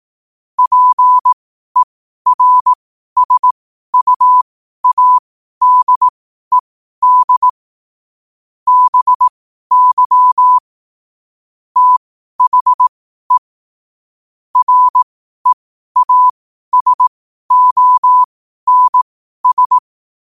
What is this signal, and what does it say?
Morse code, heard as a keyed tone, about 18 wpm, spaced as for 12 wpm, 1000 Hz. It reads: PERSUADED BY THE REASONS
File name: qotdfiles / 12wpm